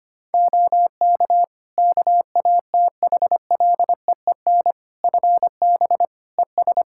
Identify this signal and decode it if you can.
Morse code, heard as a keyed tone, about 25 wpm, 700 Hz. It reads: OK KATHLEEN FB ES
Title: Practice QSO - Exchange 2 - Info - 25wpm